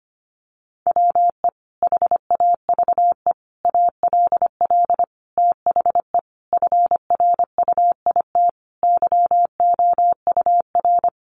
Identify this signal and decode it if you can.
Morse code, heard as a keyed tone, about 25 wpm, 700 Hz. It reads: WE HAVE ALL THE FRUIT YOUR
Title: Sentences from Top 1300 Words 25wpm